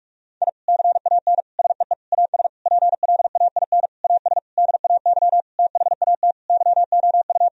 Morse code, heard as a keyed tone, about 46 wpm, 700 Hz. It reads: I CAN SEE AS PLAIN AS DAY THAT YOU
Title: Sentences from Top 1000 Words - Review Part 2 - 45wpm